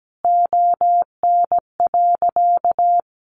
Morse code, heard as a keyed tone, about 17 wpm, 700 Hz. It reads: ON.
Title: Easy Phrases 17wpm